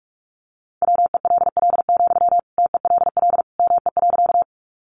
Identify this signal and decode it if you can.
Morse code, heard as a keyed tone, about 44 wpm, 700 Hz. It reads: WELL, TELL ME.
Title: Sentences from Top 200 Words 45wpm